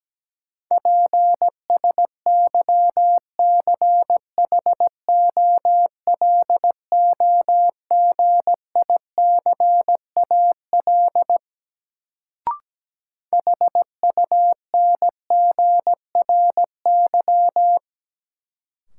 Morse code, heard as a keyed tone, about 17 wpm, 700 Hz. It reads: PSYCHOLOGICAL HUNGRY
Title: Sentences from Top 1400 Words - New Word Drill - 17wpm